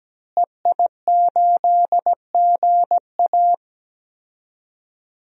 Morse code, heard as a keyed tone, about 17 wpm, 700 Hz. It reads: EI8GA
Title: DXCC-100 Call Signs - 5 Characters - Repeated 3x - 17wpm